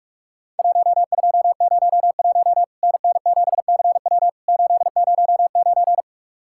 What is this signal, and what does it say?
Morse code, heard as a keyed tone, about 46 wpm, 700 Hz. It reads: 1201 NN7CW 809
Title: Worked All Europe DX Contest - QTC Exchange - 45wpm